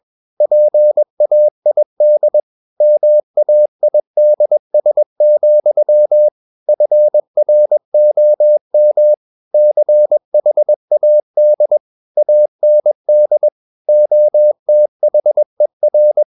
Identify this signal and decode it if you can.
Morse code, heard as a keyed tone, about 21 wpm, 600 Hz. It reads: PAID MAIDS, FROM CHAD AND OTHER
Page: CW med Gnister